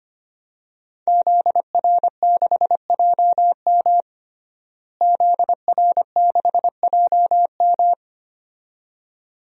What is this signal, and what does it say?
Morse code, heard as a keyed tone, about 25 wpm, 700 Hz. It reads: ZR6JM ZR6JM
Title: DXCC-100 Call Signs - 5 Characters - Repeated 2x - 25wpm